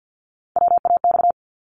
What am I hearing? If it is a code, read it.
Morse code, heard as a keyed tone, about 50 wpm, 700 Hz. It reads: WAX
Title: Multi-letter Phonemes Repeated 3x 50wpm